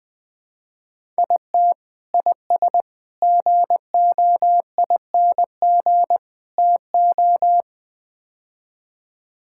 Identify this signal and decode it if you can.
Morse code, heard as a keyed tone, about 20 wpm, 700 Hz. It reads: IT IS GOING TO
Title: Build A Phrase - 5-Word Phrases - 20wpm